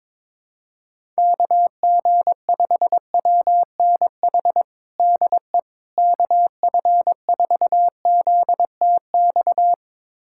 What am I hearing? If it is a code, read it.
Morse code, heard as a keyed tone, about 22 wpm, 700 Hz. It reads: KG5WNH DE KF4ZTX
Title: USA Callsign DE Callsign 22wpm